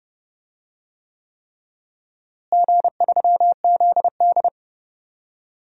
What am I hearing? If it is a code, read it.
Morse code, heard as a keyed tone, about 30 wpm, 700 Hz. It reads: G3ZD
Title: DXCC-100 Call Signs - 4 Characters - Repeated 3x - 30wpm